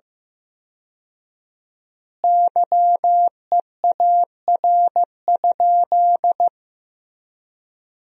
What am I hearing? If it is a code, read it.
Morse code, heard as a keyed tone, about 15 wpm, 700 Hz. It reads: YEAR?
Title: 4x Word-Spacing - Sentences from Top 200 Words - 15wpm